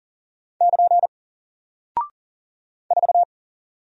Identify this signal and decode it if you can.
Morse code, heard as a keyed tone, about 40 wpm, 700 Hz. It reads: <KN> 4
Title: Single Character (Letter:Number:Punctuation:Prosign) 40wpm